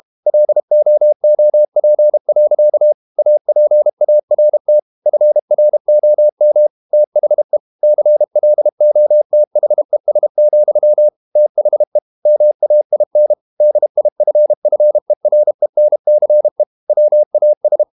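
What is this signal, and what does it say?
Morse code, heard as a keyed tone, about 32 wpm, 600 Hz. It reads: LOOP. APART FROM THE CLOTHES, THE MAIN DIFFERENCE WAS
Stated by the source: Morse code practice at 32 wpm from CW med Gnister, reading Ken Follett's novel Never in English.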